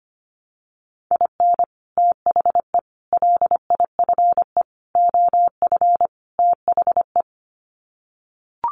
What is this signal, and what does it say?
Morse code, heard as a keyed tone, about 25 wpm, 700 Hz. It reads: IN THE LIFE OF THE
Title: Sets of 5 Words Encore2 25wpm